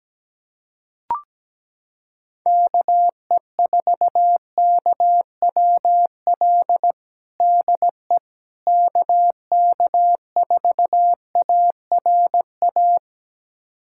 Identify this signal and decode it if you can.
Morse code, heard as a keyed tone, about 17 wpm, 700 Hz. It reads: KE4KWL DE KK4ARA
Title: USA Callsign DE Callsign - 6 Characters - 17wpm